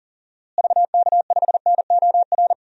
Morse code, heard as a keyed tone, about 37 wpm, 700 Hz. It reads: VK5NOR
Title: DXCC-100 Call Signs - 6 Characters - Repeated 3x - 40wpm